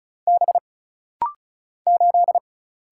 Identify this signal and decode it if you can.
Morse code, heard as a keyed tone, about 35 wpm, 700 Hz. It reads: B 8
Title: Single Letter-Number Alt - Rapid-Fire 35wpm